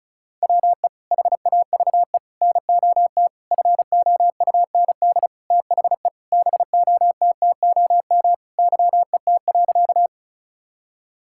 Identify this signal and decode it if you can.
Morse code, heard as a keyed tone, about 36 wpm, 700 Hz. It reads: WE HAVE NOT FOUND THE BOTTOM YET.
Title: Sentences from Top 1000 Words - Review Part 6 - 35wpm